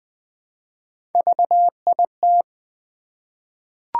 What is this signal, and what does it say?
Morse code, heard as a keyed tone, about 21 wpm, 700 Hz. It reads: VIT
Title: Three Letter-Number Combinations - 20wpm